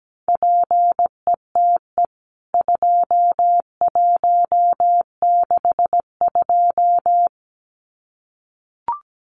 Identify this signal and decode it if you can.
Morse code, heard as a keyed tone, about 17 wpm, 700 Hz. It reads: PETE 2162
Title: CWT Exchange 17wpm